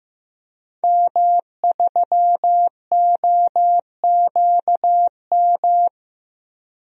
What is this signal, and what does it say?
Morse code, heard as a keyed tone, about 15 wpm, 700 Hz. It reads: M3OQM
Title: DXCC-100 Call Signs - 5 Characters - Repeated 3x - 15wpm